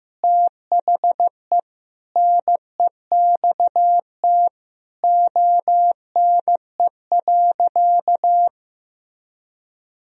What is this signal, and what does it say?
Morse code, heard as a keyed tone, about 16 wpm, 700 Hz. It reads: THE NEXT ONE.
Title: Sentences from Top 1000 Words - Review Part 1 - 15wpm